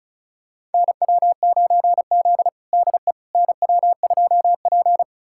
Morse code, heard as a keyed tone, about 36 wpm, 700 Hz. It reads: NW9Z DE NW2P
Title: USA Callsign DE Callsign - 4 Characters - 35wpm